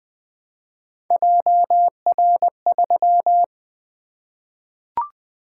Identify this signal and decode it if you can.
Morse code, heard as a keyed tone, about 20 wpm, 700 Hz. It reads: JR3
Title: Three Letter-Number Combinations - 20wpm